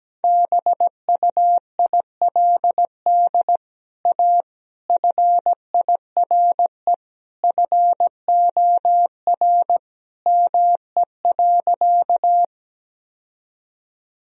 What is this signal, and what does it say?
Morse code, heard as a keyed tone, about 17 wpm, 700 Hz. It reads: BUILD A FIRE FOR ME.
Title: Sentences from Top 1000 Words - Review Part 1 - 17wpm